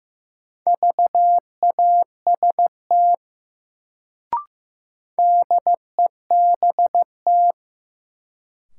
Morse code, heard as a keyed tone, about 15 wpm, 700 Hz. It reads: VAST DEBT
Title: Sentences from Top 1300 Words - New Word Drill Alt - 15wpm